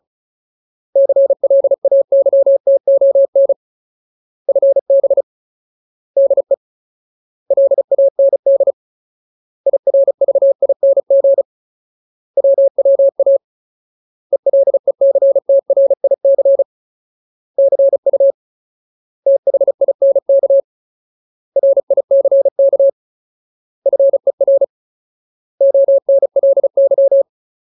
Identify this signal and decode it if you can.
Morse code, heard as a keyed tone, about 35 wpm, 550 Hz. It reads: CLAYTON FB DE LAND IRVING WWA ELECTRIC CU THINK RICK FER ONLY
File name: Ultimate Word List / 35WPM Ultimate